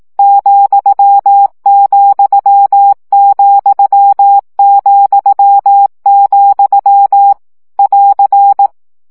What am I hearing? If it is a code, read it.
Morse code, heard as a keyed tone, about 18 wpm, 800 Hz. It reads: ,,,,, <AR>
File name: CW gyakorlas vesszo 15 baud